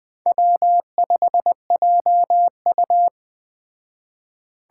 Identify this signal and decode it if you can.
Morse code, heard as a keyed tone, about 20 wpm, 700 Hz. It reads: W5JU